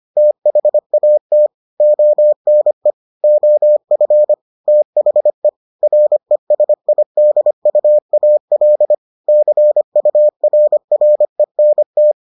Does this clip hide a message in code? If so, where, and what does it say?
Yes: Morse code from 0.2 to 12.1 s, heard as a keyed tone, about 25 wpm, 600 Hz. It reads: THAT ONE OF THE RESIDUAL CURRENT